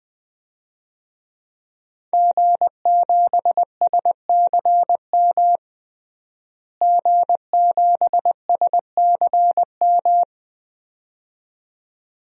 Morse code, heard as a keyed tone, about 20 wpm, 700 Hz. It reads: G7SCM G7SCM
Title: DXCC-100 Call Signs - 5 Characters - Repeated 2x - 20wpm